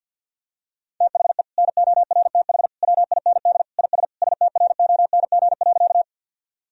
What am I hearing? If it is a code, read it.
Morse code, heard as a keyed tone, about 49 wpm, 700 Hz. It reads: THE NORTH WIND IS STRONG.
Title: Sentences from Top 1000 Words - Review Part 2 - 50wpm